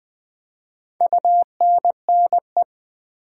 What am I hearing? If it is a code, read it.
Morse code, heard as a keyed tone, about 20 wpm, 700 Hz. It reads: UNNE
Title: Build A Word - 4 Letters At A Time - 11-Letter Words - 20wpm